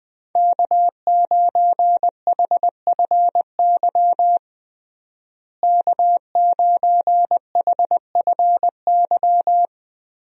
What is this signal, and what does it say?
Morse code, heard as a keyed tone, about 20 wpm, 700 Hz. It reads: K9HFY K9HFY
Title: USA Call Signs - 5 Characters - Repeated 2x - 20wpm